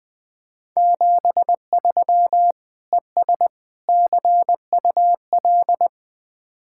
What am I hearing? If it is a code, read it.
Morse code, heard as a keyed tone, about 20 wpm, 700 Hz. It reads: 73 ES CUL